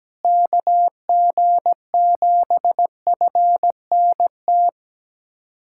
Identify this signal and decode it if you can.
Morse code, heard as a keyed tone, about 17 wpm, 700 Hz. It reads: KG7FNT
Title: USA Callsign DE Callsign - 6 Characters - 17wpm